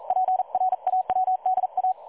Morse code, heard as a keyed tone, about 39 wpm, 750 Hz. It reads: PRAWDA
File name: dlazaawansowanych / nagraniazpasma / websdr recording 3565.0kHz